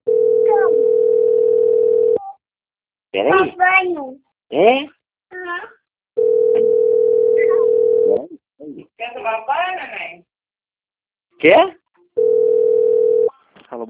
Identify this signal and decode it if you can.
Ringback tone (North America), 440 + 480 Hz, with speech.
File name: out- - - - - .38860